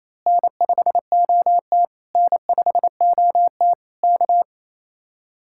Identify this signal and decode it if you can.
Morse code, heard as a keyed tone, about 26 wpm, 700 Hz. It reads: N5OT N5OT K
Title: Speed-Racing Common QSO Phrases Encore - 28wpm